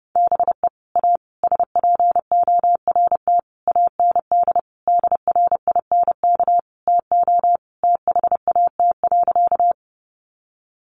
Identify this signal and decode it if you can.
Morse code, heard as a keyed tone, about 30 wpm, 700 Hz. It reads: BE A SPORT AND DRINK TO THAT.
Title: Sentences from Top 1500 Words 30wpm